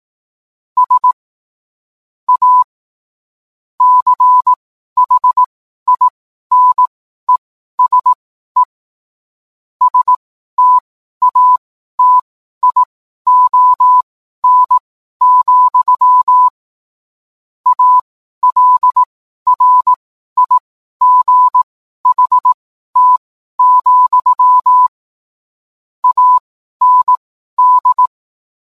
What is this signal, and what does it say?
Morse code, heard as a keyed tone, about 18 wpm, spaced as for 12 wpm, 1000 Hz. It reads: S A CHINESE STATION, ALRIGHT, AND